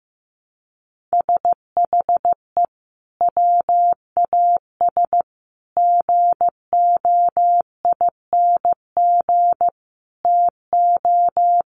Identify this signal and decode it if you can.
Morse code, heard as a keyed tone, about 15 wpm, 700 Hz. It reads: SHE WAS GOING TO
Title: Sets of 4 Words 15wpm